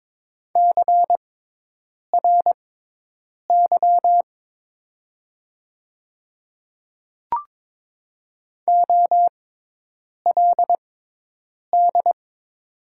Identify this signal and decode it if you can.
Morse code, heard as a keyed tone, about 22 wpm, spaced as for 8 wpm, 700 Hz. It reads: CRY OLD
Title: 6x Character Spacing - Top 500 Words - 22wpm